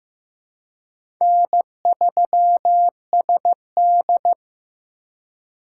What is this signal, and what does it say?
Morse code, heard as a keyed tone, about 15 wpm, 700 Hz. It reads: N3SD
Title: USA Call Signs - 4 Characters - 4x Character Spacing - 15wpm